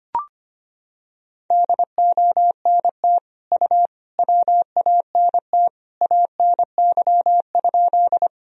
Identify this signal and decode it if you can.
Morse code, heard as a keyed tone, about 25 wpm, 700 Hz. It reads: DONT U WANT ANY?
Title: Sentences from Top 300 Words and Abbreviations 25wpm